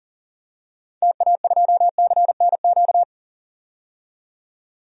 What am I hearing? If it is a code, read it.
Morse code, heard as a keyed tone, about 40 wpm, 700 Hz. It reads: TA2CNQ
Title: DXCC-100 Call Signs - 6 Characters - 40wpm